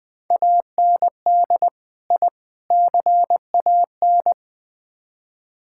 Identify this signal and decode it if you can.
Morse code, heard as a keyed tone, about 20 wpm, 700 Hz. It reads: AND I CAN
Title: Build A Phrase - 4-Word Phrases - 20wpm